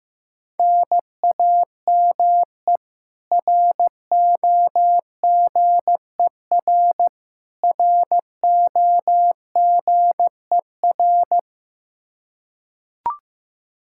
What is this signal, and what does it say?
Morse code, heard as a keyed tone, about 15 wpm, 700 Hz. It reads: NAME ROGER ROGER
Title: Practice QSO - Exchange 1 - Beginning - 15wpm